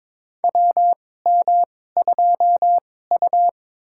Morse code, heard as a keyed tone, about 23 wpm, spaced as for 16 wpm, 700 Hz. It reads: WM2U
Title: USA Call Signs - 4 Characters - 2x Character Spacing - 22wpm